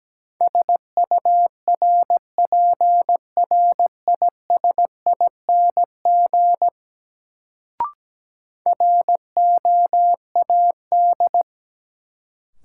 Morse code, heard as a keyed tone, about 17 wpm, 700 Hz. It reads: SURPRISING ROAD
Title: Sets of 5 Words - New Word Drill - 17wpm